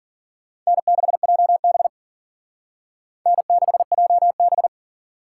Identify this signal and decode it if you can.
Morse code, heard as a keyed tone, about 51 wpm, 700 Hz. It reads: N6JB N6JB